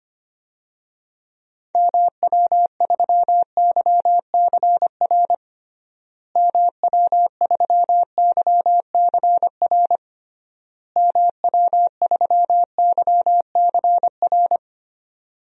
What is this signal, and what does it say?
Morse code, heard as a keyed tone, about 25 wpm, 700 Hz. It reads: MW3YCR MW3YCR MW3YCR
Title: DXCC-100 Call Signs - 6 Characters - Repeated 3x - 25wpm